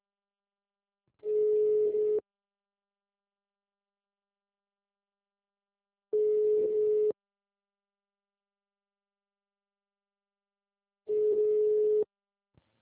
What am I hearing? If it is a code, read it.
Ringback tone (Europe, 425 Hz).